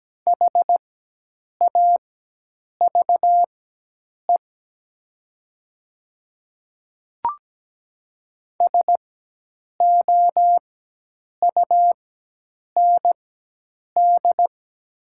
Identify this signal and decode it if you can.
Morse code, heard as a keyed tone, about 17 wpm, spaced as for 8 wpm, 690 Hz. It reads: HAVE SOUND
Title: 4x Character Spacing - Top 200 Words - 17wpm